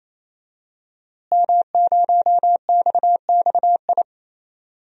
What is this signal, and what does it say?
Morse code, heard as a keyed tone, about 28 wpm, 700 Hz. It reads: M0XXI
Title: DXCC-100 Call Signs - 5 Characters - Repeated 3x - 28wpm